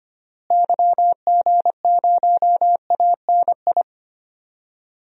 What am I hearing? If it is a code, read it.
Morse code, heard as a keyed tone, about 25 wpm, 700 Hz. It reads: YG0ANI
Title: DXCC-100 Call Signs - 6 Characters - Repeated 3x - 25wpm